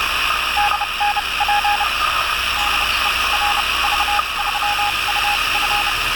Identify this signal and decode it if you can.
Morse code, heard as a keyed tone, about 31 wpm, spaced as for 24 wpm, 820 Hz, with radio noise. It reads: DNP DE RV3UF
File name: attachments / 20.07.2017 RV3UF RX UA1CEG Sputnik